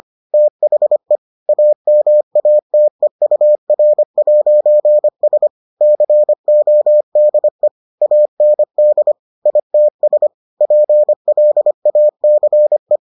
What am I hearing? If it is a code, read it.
Morse code, heard as a keyed tone, about 25 wpm, 600 Hz. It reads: THE AMATEUR'S CODE AND ITS PLACE